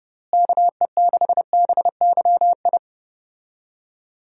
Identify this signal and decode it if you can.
Morse code, heard as a keyed tone, about 32 wpm, 700 Hz. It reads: KE6BYI
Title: USA Call Signs - 6 Characters - Repeated 3x - 30wpm